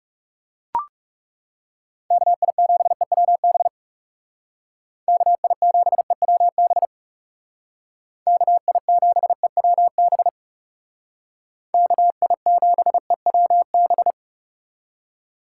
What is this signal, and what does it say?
Morse code, heard as a keyed tone, about 39 wpm, 700 Hz. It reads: KI7EWB KI7EWB KI7EWB KI7EWB